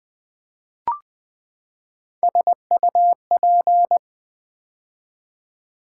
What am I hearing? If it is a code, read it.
Morse code, heard as a keyed tone, about 20 wpm, 700 Hz. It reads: SUP
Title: Build A Word - 3 Letters At A Time - 6-Letter Words - 20wpm